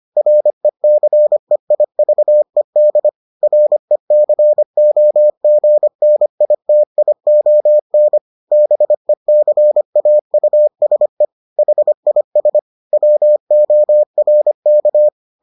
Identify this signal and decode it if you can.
Morse code, heard as a keyed tone, about 25 wpm, 600 Hz. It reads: RECEIVED RECOGNITION BECAUSE HIS WORK